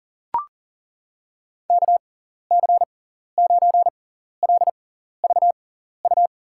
Morse code, heard as a keyed tone, about 40 wpm, spaced as for 14 wpm, 700 Hz. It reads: KC9LVU
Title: USA Call Signs - 6 Characters - 6x Character Spacing - 40wpm